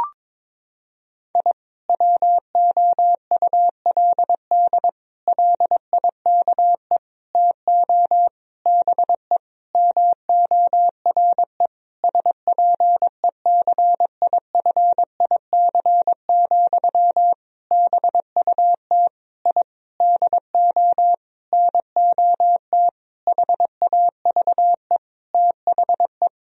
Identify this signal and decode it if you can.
Morse code, heard as a keyed tone, about 22 wpm, 700 Hz. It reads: I WOULD LIKE TO BE MORE SPECIFIC, BUT I DO NOT HAVE THE
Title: Sentences from Top 1000 Words - Review Part 3 - 22wpm